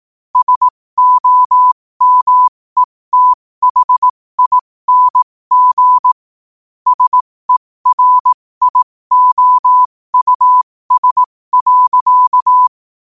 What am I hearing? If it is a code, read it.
Morse code, heard as a keyed tone, about 18 wpm, spaced as for 15 wpm, 1000 Hz. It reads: SOMETHING SERIOUS.
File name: qotdfiles / 15wpm